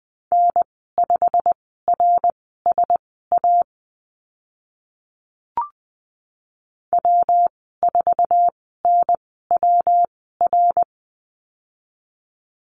Morse code, heard as a keyed tone, about 20 wpm, spaced as for 14 wpm, 700 Hz. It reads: N5RSA W4NWR
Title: 128 Lesson 21g - Number 6 - Call Signs 20wpm